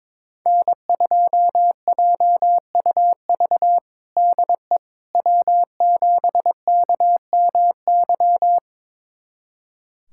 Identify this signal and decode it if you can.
Morse code, heard as a keyed tone, about 22 wpm, 700 Hz. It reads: N2JUV DE W7KMY
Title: USA Callsign DE Callsign - 5 Characters - 22wpm